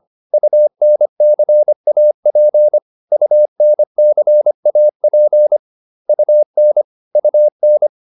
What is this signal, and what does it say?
Morse code, heard as a keyed tone, about 25 wpm, 600 Hz. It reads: UNCAP UNCAP UN UN